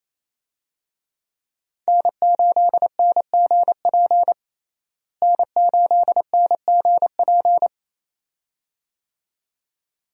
Morse code, heard as a keyed tone, about 28 wpm, 700 Hz. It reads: N8NGP N8NGP
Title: USA Call Signs - 5 Characters - Repeated 2x - 28wpm